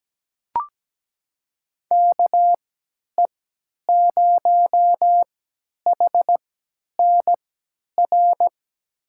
Morse code, heard as a keyed tone, about 17 wpm, spaced as for 10 wpm, 700 Hz. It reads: KE0HNR
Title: USA Call Signs - 6 Characters - 3x Character Spacing - 17wpm